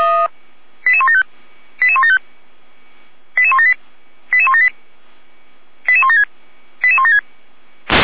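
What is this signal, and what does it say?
Five-tone selective call (ZVEI), heard as quick runs of five tones, tones of about 80 ms. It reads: 88175 88175 88178 88178 88176 88176
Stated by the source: ZVEI five-tone call 88352 received by SDR.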